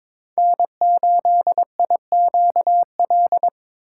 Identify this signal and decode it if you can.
Morse code, heard as a keyed tone, about 22 wpm, 700 Hz. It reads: N8IQL
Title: USA Call Signs - 5 Characters - 8x Character Spacing - 22wpm